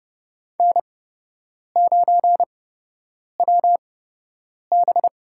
Morse code, heard as a keyed tone, about 30 wpm, spaced as for 8 wpm, 700 Hz. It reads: N9WB
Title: USA Call Signs - 4 Characters - 8x Character Spacing - 30wpm